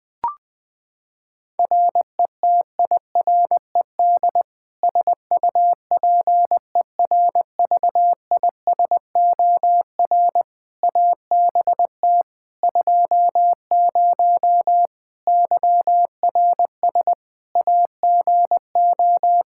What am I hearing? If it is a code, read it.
Morse code, heard as a keyed tone, about 20 wpm, 700 Hz. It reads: RETIRED SUPERVISOR ABT 20 YRS AGO
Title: Practice QSO - Exchange 3 - More Info - 20wpm